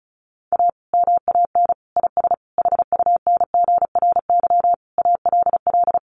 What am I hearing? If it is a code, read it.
Morse code, heard as a keyed tone, about 34 wpm, 700 Hz. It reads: A MAN IS HUNGRY ALL
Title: Sentences from Top 1400 Words 35wpm